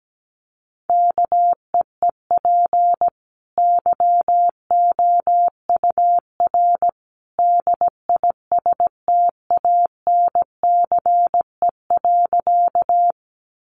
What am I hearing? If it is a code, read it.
Morse code, heard as a keyed tone, about 17 wpm, 700 Hz. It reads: KEEP YOUR DISTANCE.
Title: Sentences from Top 1200 Words 17wpm